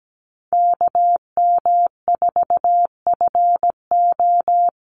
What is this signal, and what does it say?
Morse code, heard as a keyed tone, about 17 wpm, 700 Hz. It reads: KM4FO
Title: Call Signs Repeated 3x - 17wpm